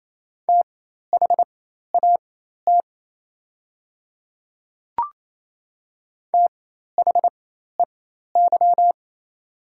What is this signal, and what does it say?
Morse code, heard as a keyed tone, about 28 wpm, spaced as for 13 wpm, 700 Hz. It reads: THAT THEY
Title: 4x Character Spacing - Top 100 Words - 28wpm